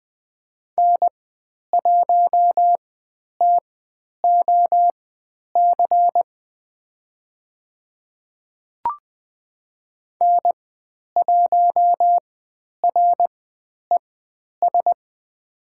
Morse code, heard as a keyed tone, about 20 wpm, spaced as for 10 wpm, 700 Hz. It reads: N1TOC N1RES
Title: 102 Lesson 17g - Letter u - Call Signs 20wpm